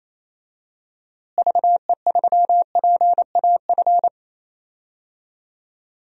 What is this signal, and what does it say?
Morse code, heard as a keyed tone, about 28 wpm, 700 Hz. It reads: VE3PAF
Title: DXCC-100 Call Signs - 6 Characters - Repeated 3x - 28wpm